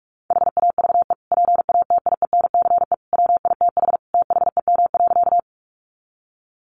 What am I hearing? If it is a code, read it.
Morse code, heard as a keyed tone, about 44 wpm, 700 Hz. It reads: HAVE PATIENCE WITH THEM.
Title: Sentences from Top 1300 Words 45wpm